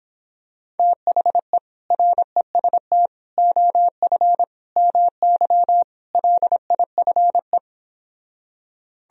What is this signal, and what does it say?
Morse code, heard as a keyed tone, about 26 wpm, 700 Hz. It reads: THE REST OF MY LIFE